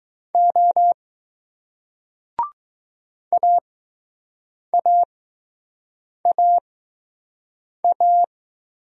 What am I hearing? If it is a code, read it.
Morse code, heard as a keyed tone, about 20 wpm, spaced as for 6 wpm, 700 Hz. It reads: O AAAA